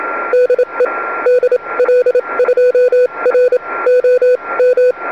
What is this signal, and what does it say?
Morse code, heard as a keyed tone, about 25 wpm, spaced as for 21 wpm, 490 Hz. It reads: DE DL2ROM
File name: oh1xt-2m